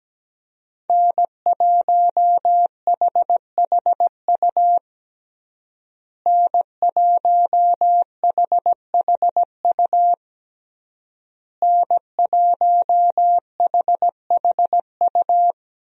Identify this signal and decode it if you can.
Morse code, heard as a keyed tone, about 17 wpm, 700 Hz. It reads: N1HHU N1HHU N1HHU
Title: USA Call Signs - 5 Characters - Repeated 3x - 17wpm